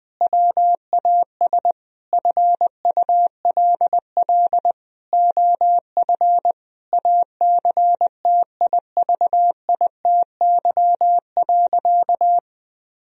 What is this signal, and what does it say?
Morse code, heard as a keyed tone, about 21 wpm, 700 Hz. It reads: WAS FULL OF ACTIVITY.
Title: Sentences from Top 1000 Words - Review Part 4 - 20wpm